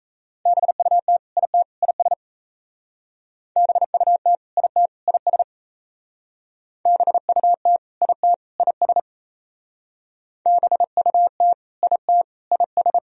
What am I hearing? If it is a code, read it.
Morse code, heard as a keyed tone, about 35 wpm, 700 Hz. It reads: BUT IT IS BUT IT IS BUT IT IS BUT IT IS